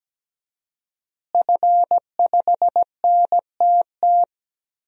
Morse code, heard as a keyed tone, about 17 wpm, 700 Hz. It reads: F5NTT
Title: DXCC-100 Call Signs - 5 Characters - Repeated 2x - 17wpm